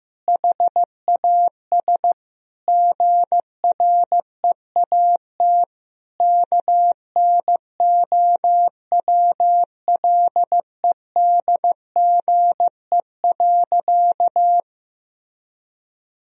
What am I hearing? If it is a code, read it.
Morse code, heard as a keyed tone, about 15 wpm, 700 Hz. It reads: HAS GREAT KNOWLEDGE.
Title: Sentences from Top 1000 Words - Review Part 3 - 15wpm